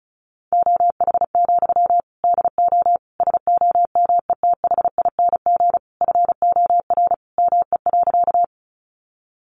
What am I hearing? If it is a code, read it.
Morse code, heard as a keyed tone, about 34 wpm, 700 Hz. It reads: OH, DO SOMETHING FOR ME.
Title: Sentences from Top 600 Words 35wpm